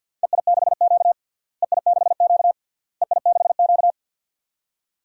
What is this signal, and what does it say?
Morse code, heard as a keyed tone, about 51 wpm, 700 Hz. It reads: EI6Q EI6Q EI6Q
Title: DXCC-100 Call Signs - 4 Characters - Repeated 3x - 50wpm